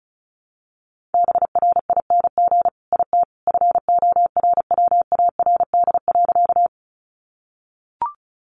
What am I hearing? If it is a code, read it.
Morse code, heard as a keyed tone, about 34 wpm, 700 Hz. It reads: BRING IT FORWARD.
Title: Sentences from Top 1300 Words 35wpm